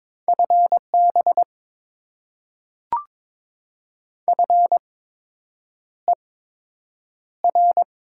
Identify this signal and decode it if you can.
Morse code, heard as a keyed tone, about 21 wpm, spaced as for 8 wpm, 700 Hz. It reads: FB FER
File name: Top 20 QSO Elements - 8x Character Spacing - 22wpm